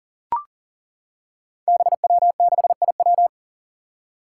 Morse code, heard as a keyed tone, about 42 wpm, 700 Hz. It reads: BW6IW
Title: Five Challenging Letter-Number Combinations 40wpm